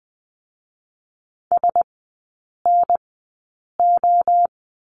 Morse code, heard as a keyed tone, about 19 wpm, 700 Hz. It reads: S N O
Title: 047 Lesson 9e - Number 4 - Three Characters 20wpm